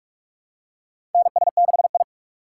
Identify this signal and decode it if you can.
Morse code, heard as a keyed tone, about 46 wpm, 700 Hz. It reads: NS6I
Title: USA Call Signs - 4 Characters - 6x Character Spacing - 45wpm